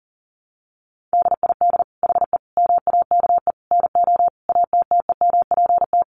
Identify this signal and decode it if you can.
Morse code, heard as a keyed tone, about 40 wpm, 700 Hz. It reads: DID HE MAKE NO ATTEMPT
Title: Sentences from Top 1000 Words 40wpm